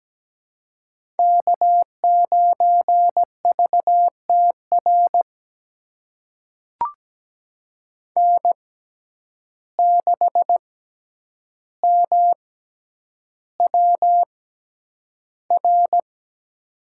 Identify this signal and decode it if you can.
Morse code, heard as a keyed tone, about 17 wpm, spaced as for 9 wpm, 700 Hz. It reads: K9VTR N6MWR
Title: USA Call Signs - 5 Characters - 6x Character Spacing - 17wpm